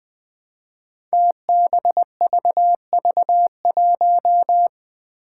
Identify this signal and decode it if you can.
Morse code, heard as a keyed tone, about 20 wpm, 700 Hz. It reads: TBVV1
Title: Five Challenging Letter-Number Combinations 20wpm